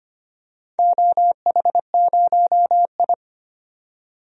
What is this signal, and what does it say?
Morse code, heard as a keyed tone, about 25 wpm, 700 Hz. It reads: OH0I
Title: DXCC-100 Call Signs - 4 Characters - 25wpm